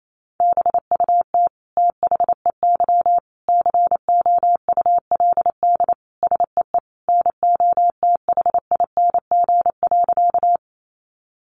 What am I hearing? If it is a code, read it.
Morse code, heard as a keyed tone, about 28 wpm, 700 Hz. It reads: BUT THEY COULD SEE NOTHING.
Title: Sentences from Top 500 Words 28wpm